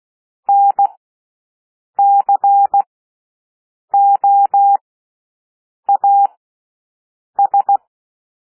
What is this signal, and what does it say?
Morse code, heard as a keyed tone, about 16 wpm, spaced as for 6 wpm, 800 Hz. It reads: NCOAS
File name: cwmp3 / L1 SOUND LETTERS 16 wpm